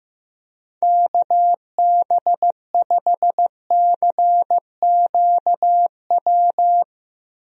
Morse code, heard as a keyed tone, about 15 wpm, 700 Hz. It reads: KB5CQW
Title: USA Call Signs - 6 Characters - Repeated 2x - 15wpm